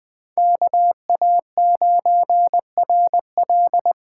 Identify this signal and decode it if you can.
Morse code, heard as a keyed tone, about 20 wpm, 700 Hz. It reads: KA9RL
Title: USA Call Signs - 8x Character Spacing - 20wpm